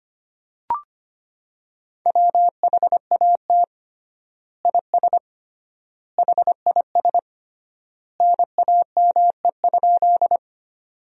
Morse code, heard as a keyed tone, about 25 wpm, 700 Hz. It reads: WHAT IS HIS NAME?
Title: 3x Word-Spacing - Sentences from Top 200 Words - 25wpm